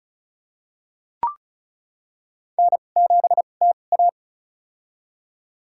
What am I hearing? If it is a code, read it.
Morse code, heard as a keyed tone, about 36 wpm, spaced as for 25 wpm, 690 Hz. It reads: N7TA
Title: USA Call Signs - 4 Characters - 2x Character Spacing - 35wpm